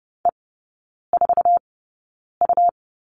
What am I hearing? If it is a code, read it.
Morse code, heard as a keyed tone, about 30 wpm, spaced as for 9 wpm, 690 Hz. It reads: E4U
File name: Lesson 17e - Letter u - Three Characters 30wpm